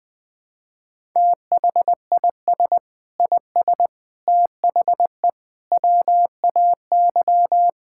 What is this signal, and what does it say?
Morse code, heard as a keyed tone, about 21 wpm, 700 Hz. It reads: THIS IS THE WAY
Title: Sentences with 4-Word Phrases - 20wpm